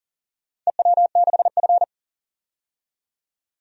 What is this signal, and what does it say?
Morse code, heard as a keyed tone, about 38 wpm, 700 Hz. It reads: EW6F
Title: DXCC-100 Call Signs - 4 Characters - 40wpm